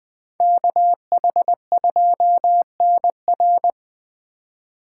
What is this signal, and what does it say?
Morse code, heard as a keyed tone, about 21 wpm, 700 Hz. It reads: KH2NR
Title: Speed Racing - USA 2x2 Call Signs 17wpm